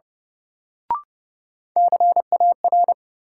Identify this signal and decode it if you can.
Morse code, heard as a keyed tone, about 30 wpm, 700 Hz. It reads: CAR
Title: Sets of 5 Words Encore - New Word Drill - 30wpm